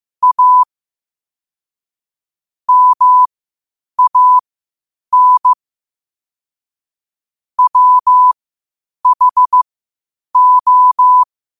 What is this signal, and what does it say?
Morse code, heard as a keyed tone, about 15 wpm, spaced as for 8 wpm, 1000 Hz. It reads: A MAN WHO
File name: qotdfiles / 8wpm